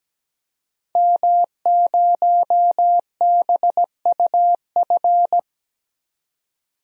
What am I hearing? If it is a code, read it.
Morse code, heard as a keyed tone, about 17 wpm, 700 Hz. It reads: M0BUF
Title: DXCC-100 Call Signs - 5 Characters - 17wpm